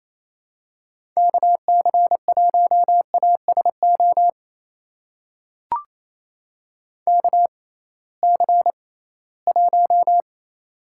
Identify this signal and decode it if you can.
Morse code, heard as a keyed tone, about 28 wpm, spaced as for 18 wpm, 700 Hz. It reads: KC1ASO KC1
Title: USA Call Signs - 6 Characters - 6x Character Spacing - 28wpm